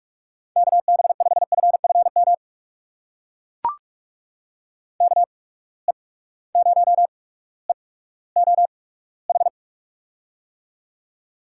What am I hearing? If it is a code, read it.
Morse code, heard as a keyed tone, about 44 wpm, spaced as for 19 wpm, 700 Hz. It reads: KB5FFM KE0EOH
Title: USA Call Signs - 6 Characters - 8x Character Spacing - 45wpm